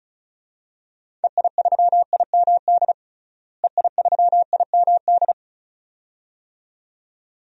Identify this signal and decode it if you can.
Morse code, heard as a keyed tone, about 36 wpm, 700 Hz. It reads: EI3IMD EI3IMD
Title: DXCC-100 Call Signs - 6 Characters - Repeated 2x - 35wpm